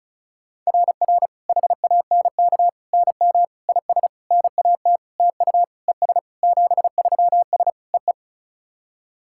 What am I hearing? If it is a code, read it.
Morse code, heard as a keyed tone, about 34 wpm, 690 Hz. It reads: RR HANK NM IS NAT TU ES 73S EE
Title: SOTA QSO Practice 35wpm